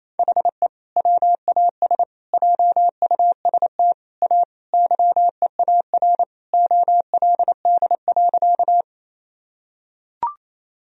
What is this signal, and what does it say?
Morse code, heard as a keyed tone, about 28 wpm, 700 Hz. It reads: HE WAS JUST A YEAR OLD.
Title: Sentences from Top 1000 Words - Review Part 1 - 28wpm